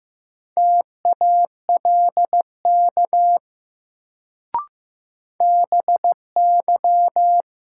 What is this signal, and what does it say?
Morse code, heard as a keyed tone, about 15 wpm, 700 Hz. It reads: TALK BY
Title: Sets of 2 Words - New Word Drill - 15wpm